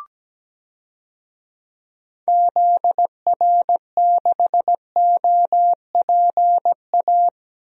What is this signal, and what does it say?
Morse code, heard as a keyed tone, about 17 wpm, 700 Hz. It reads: ZR6OPA
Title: DXCC-100 Call Signs - 6 Characters - Repeated 2x - 17wpm